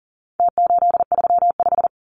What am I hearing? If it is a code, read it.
Morse code, heard as a keyed tone, about 40 wpm, 700 Hz. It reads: T835
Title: Numbers - Contest Style - 4-Digit 40wpm